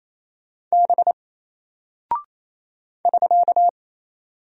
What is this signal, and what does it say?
Morse code, heard as a keyed tone, about 28 wpm, 700 Hz. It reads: B <SK>
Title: Single Character (Letter:Number:Punctuation:Prosign) Alt 28wpm